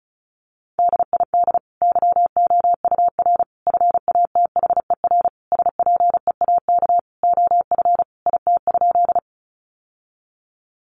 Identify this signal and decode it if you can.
Morse code, heard as a keyed tone, about 34 wpm, 700 Hz. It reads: DID YOUR FATHER SPEAK OF IT?
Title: Sentences from Top 1100 Words 35wpm